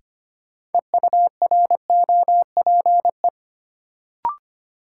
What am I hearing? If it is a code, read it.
Morse code, heard as a keyed tone, about 25 wpm, 700 Hz. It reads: EUROPE
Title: Sets of 5 Words Encore - New Word Drill - 25wpm